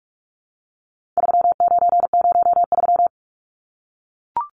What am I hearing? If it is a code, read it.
Morse code, heard as a keyed tone, about 44 wpm, 700 Hz. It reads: 3903
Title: Numbers - 4-Digit 45wpm